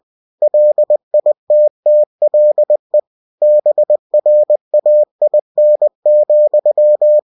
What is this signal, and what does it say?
Morse code, heard as a keyed tone, about 20 wpm, 600 Hz. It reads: LITTLE BRAIN,
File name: cw-books / winnie-the-pooh / winnie-20wpm-page-0029